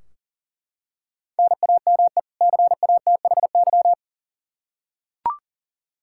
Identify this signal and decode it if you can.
Morse code, heard as a keyed tone, about 40 wpm, 700 Hz. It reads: NAME CATHY
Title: Practice QSO - Exchange 1 - Beginning - 40wpm